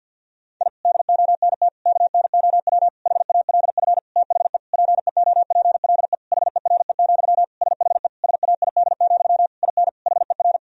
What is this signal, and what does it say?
Morse code, heard as a keyed tone, about 47 wpm, 700 Hz. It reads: I DONT KNOW HALF THE PEOPLE HERE, SHE SAID, IN HER
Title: Sentences from Top 1000 Words - Review Part 1 - 50wpm